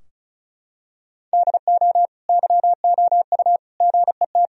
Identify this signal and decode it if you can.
Morse code, heard as a keyed tone, about 36 wpm, 700 Hz. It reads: DO YOU GET
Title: Sentences from Top 1000 Words - Review Part 4 - 35wpm